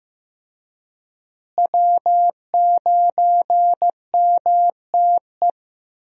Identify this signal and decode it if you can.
Morse code, heard as a keyed tone, about 15 wpm, 700 Hz. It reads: W9MTE
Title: USA Call Signs - 5 Characters - Repeated 2x - 15wpm